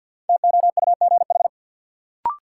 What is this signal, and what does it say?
Morse code, heard as a keyed tone, about 51 wpm, 700 Hz. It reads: TOUGH
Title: Sentences from Top 1200 Words - New Word Drill Alt - 50wpm